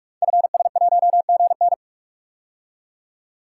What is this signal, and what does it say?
Morse code, heard as a keyed tone, about 44 wpm, 700 Hz. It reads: FS1GN
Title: Code Groups - 5 Letter:Numbers - 45wpm